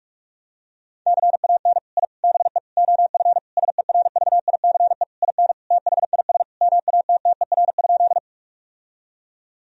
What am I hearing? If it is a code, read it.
Morse code, heard as a keyed tone, about 46 wpm, 700 Hz. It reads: CAN I BE OF SERVICE IN THIS MATTER?
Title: Sentences from Top 1000 Words - Review Part 4 - 45wpm